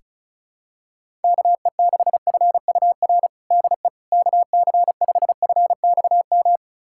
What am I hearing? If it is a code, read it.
Morse code, heard as a keyed tone, about 36 wpm, 690 Hz. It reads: KE6FUR DE KC5FXM
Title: USA Callsign DE Callsign 35wpm